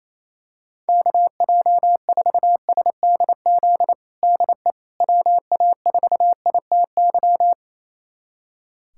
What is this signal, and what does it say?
Morse code, heard as a keyed tone, about 28 wpm, 700 Hz. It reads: KJ4SDZ DE WA4ITY
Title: USA Callsign DE Callsign - 6 Characters - 28wpm